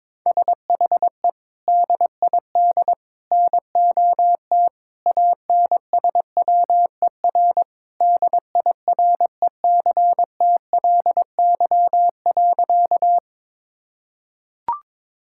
Morse code, heard as a keyed tone, about 22 wpm, 700 Hz. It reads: SHE DID NOT ANSWER DIRECTLY.
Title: Sentences from Top 1000 Words - Review Part 5 - 22wpm